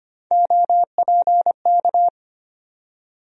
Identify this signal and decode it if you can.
Morse code, heard as a keyed tone, about 25 wpm, 700 Hz. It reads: OPK
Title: USA Call Sign Postfixes - 3 Letters 25wpm